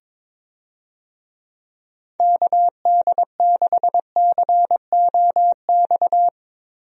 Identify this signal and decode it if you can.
Morse code, heard as a keyed tone, about 21 wpm, 700 Hz. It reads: KD6COX
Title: USA Call Signs - 6 Characters - Repeated 2x - 22wpm